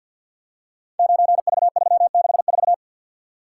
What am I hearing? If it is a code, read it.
Morse code, heard as a keyed tone, about 46 wpm, 700 Hz. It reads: 9V364
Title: Code Groups - 5 Letter:Numbers - 50wpm